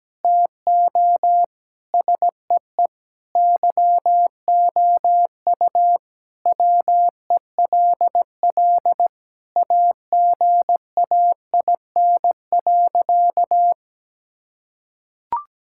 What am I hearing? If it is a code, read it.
Morse code, heard as a keyed tone, about 17 wpm, 700 Hz. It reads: TO SEE YOU WELL AGAIN.
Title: Sentences from Top 1600 Words 17wpm